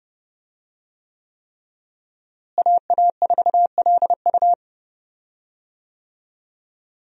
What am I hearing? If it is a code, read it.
Morse code, heard as a keyed tone, about 30 wpm, 700 Hz. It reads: AA4LU
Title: USA Call Signs - 5 Characters - Repeated 3x - 30wpm